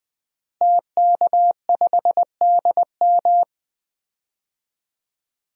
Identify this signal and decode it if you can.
Morse code, heard as a keyed tone, about 20 wpm, 700 Hz. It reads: TK5DM
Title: DXCC-100 Call Signs - 5 Characters - Repeated 2x - 20wpm